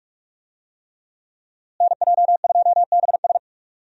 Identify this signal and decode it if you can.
Morse code, heard as a keyed tone, about 42 wpm, 700 Hz. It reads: NJ2BS
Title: USA Call Signs - 5 Characters - Repeated 3x - 45wpm